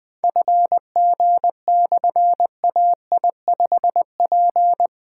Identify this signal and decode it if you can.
Morse code, heard as a keyed tone, about 21 wpm, 700 Hz. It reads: FG/AI5P
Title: Difficult Call Signs 20wpm